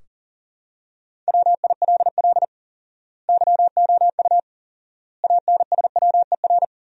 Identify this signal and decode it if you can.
Morse code, heard as a keyed tone, about 40 wpm, 700 Hz. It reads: WILL YOU ANSWER
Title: 4x Word-Spacing - Sentences from Top 400 Words - 40wpm